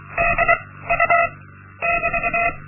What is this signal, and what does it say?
Morse code, heard as a keyed tone, about 23 wpm, 690 Hz, with noise. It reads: D U <BT>
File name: Linksmakalnis-1